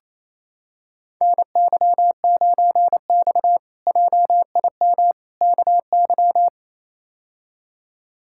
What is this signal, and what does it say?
Morse code, heard as a keyed tone, about 28 wpm, 700 Hz. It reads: NY9X JIM KY
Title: K1USN SST QSO Practice - 28wpm